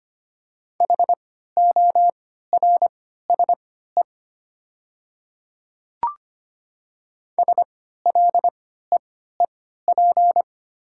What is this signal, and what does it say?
Morse code, heard as a keyed tone, about 25 wpm, spaced as for 14 wpm, 700 Hz. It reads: HORSE SLEEP
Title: 3x Character Spacing - Top 500 Words - 25wpm